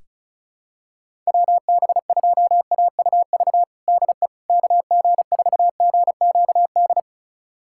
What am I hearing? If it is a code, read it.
Morse code, heard as a keyed tone, about 34 wpm, 700 Hz. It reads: WB2AUV DE KG4GQD
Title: USA Callsign DE Callsign - 6 Characters - 35wpm